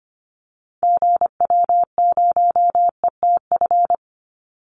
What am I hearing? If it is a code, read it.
Morse code, heard as a keyed tone, about 25 wpm, 700 Hz. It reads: GW0ETF
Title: Call Signs - 25wpm